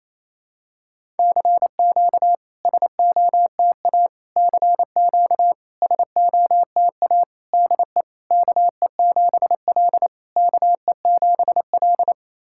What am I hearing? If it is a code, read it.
Morse code, heard as a keyed tone, about 28 wpm, 690 Hz. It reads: CQ SOTA CQ SOTA DE KE7L KE7L
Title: SOTA QSO Practice 28wpm